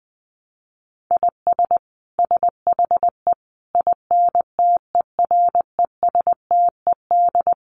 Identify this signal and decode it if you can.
Morse code, heard as a keyed tone, about 20 wpm, 700 Hz. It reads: IS SHE INTERESTED
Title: Sentences from Top 900 Words 20wpm